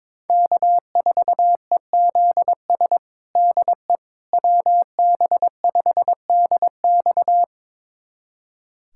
Morse code, heard as a keyed tone, about 22 wpm, 700 Hz. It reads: K4EZS DE WB5DX
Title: USA Callsign DE Callsign - 5 Characters - 22wpm